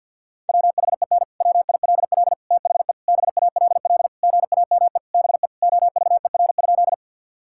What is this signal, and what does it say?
Morse code, heard as a keyed tone, about 50 wpm, 700 Hz. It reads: WHEN WILL THE BALL GAME BE OVER?
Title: Sentences from Top 1000 Words - Review Part 6 - 50wpm